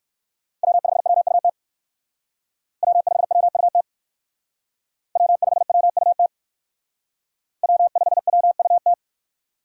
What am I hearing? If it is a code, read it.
Morse code, heard as a keyed tone, about 57 wpm, 700 Hz. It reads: W5WUT W5WUT W5WUT W5WUT